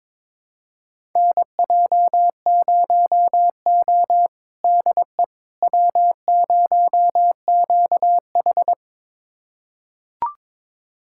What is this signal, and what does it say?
Morse code, heard as a keyed tone, about 22 wpm, 700 Hz. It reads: NJ0O DE W0QH
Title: USA Callsign DE Callsign - 4 Characters - 22wpm